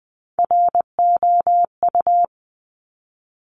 Morse code, heard as a keyed tone, about 20 wpm, 700 Hz. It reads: ROU
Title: Top Three Letter Combinations in Words 20wpm